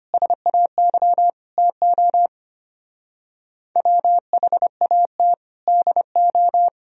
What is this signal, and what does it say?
Morse code, heard as a keyed tone, about 32 wpm, 700 Hz. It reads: SAY TO WHAT DO